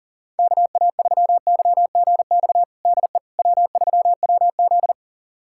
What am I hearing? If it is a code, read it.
Morse code, heard as a keyed tone, about 40 wpm, 700 Hz. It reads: KA3YGX DE W3WZ
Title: USA Callsign DE Callsign 40wpm